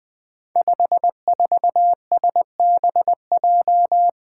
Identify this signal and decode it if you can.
Morse code, heard as a keyed tone, about 21 wpm, 700 Hz. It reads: 54SBJ
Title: Five Challenging Letter-Number Combinations 20wpm